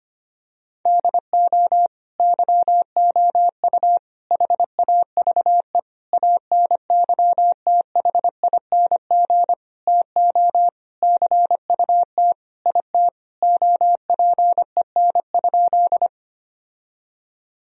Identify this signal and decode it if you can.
Morse code, heard as a keyed tone, about 25 wpm, 700 Hz. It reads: DO YOU HAVE ANYTHING TO CUT IT OPEN?
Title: Sentences from Top 1000 Words - Review Part 1 - 25wpm